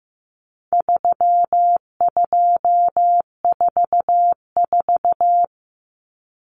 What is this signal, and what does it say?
Morse code, heard as a keyed tone, about 15 wpm, 700 Hz. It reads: 3244
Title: Common Cut Numbers - 4-Digit 15wpm